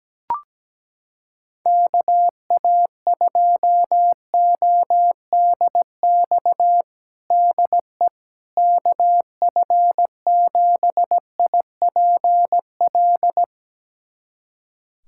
Morse code, heard as a keyed tone, about 17 wpm, 700 Hz. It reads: KA2ODX DE KF7IPL
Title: USA Callsign DE Callsign - 6 Characters - 17wpm